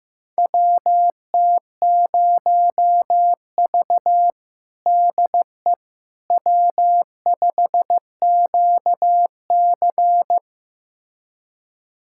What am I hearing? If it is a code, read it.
Morse code, heard as a keyed tone, about 16 wpm, 700 Hz. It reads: WT0V DE W5QC
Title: USA Callsign DE Callsign - 4 Characters - 15wpm